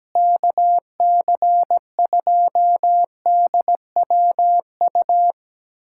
Morse code, heard as a keyed tone, about 17 wpm, 700 Hz. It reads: KC2DWU
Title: USA Callsign DE Callsign - 6 Characters - 17wpm